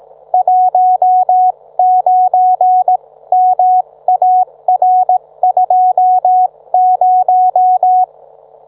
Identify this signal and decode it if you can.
Morse code, heard as a keyed tone, about 18 wpm, spaced as for 15 wpm, 740 Hz. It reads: 19MAR20